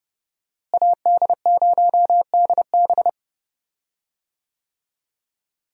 Morse code, heard as a keyed tone, about 30 wpm, 700 Hz. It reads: AD0DB
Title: USA Call Signs - 5 Characters - Repeated 2x - 30wpm